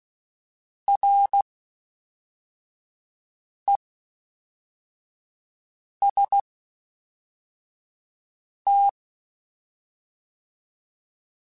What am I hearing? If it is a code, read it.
Morse code, heard as a keyed tone, about 16 wpm, 800 Hz. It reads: R E S T
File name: Combinations / Combinations-mp3s / Combinations@10